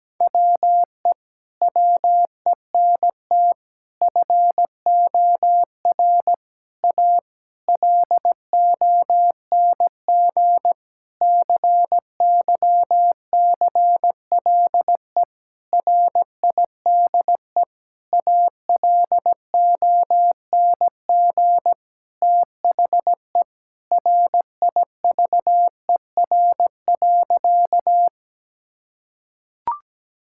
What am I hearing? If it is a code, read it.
Morse code, heard as a keyed tone, about 17 wpm, 700 Hz. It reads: WE WENT FOR A LONG CYCLE RIDE ALONG THE RIVER.
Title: Sentences from Top 1000 Words - Review Part 6 - 17wpm